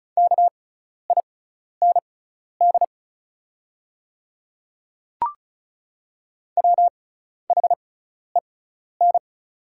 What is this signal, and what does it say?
Morse code, heard as a keyed tone, about 34 wpm, spaced as for 12 wpm, 700 Hz. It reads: KIND WHEN
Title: 6x Character Spacing - Top 200 Words - 35wpm